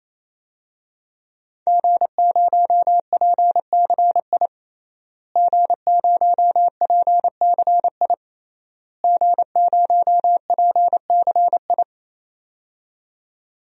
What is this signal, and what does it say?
Morse code, heard as a keyed tone, about 27 wpm, 700 Hz. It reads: G0PCI G0PCI G0PCI
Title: DXCC-100 Call Signs - 5 Characters - Repeated 3x - 28wpm